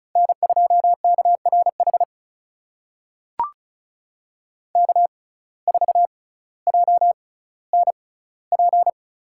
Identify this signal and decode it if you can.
Morse code, heard as a keyed tone, about 34 wpm, spaced as for 18 wpm, 700 Hz. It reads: N2KRH K4JNP
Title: USA Call Signs - 5 Characters - 6x Character Spacing - 35wpm